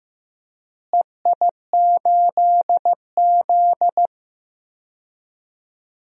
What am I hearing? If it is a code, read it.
Morse code, heard as a keyed tone, about 16 wpm, 700 Hz. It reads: EI8Z
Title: DXCC-100 Call Signs - 4 Characters - Repeated 3x - 15wpm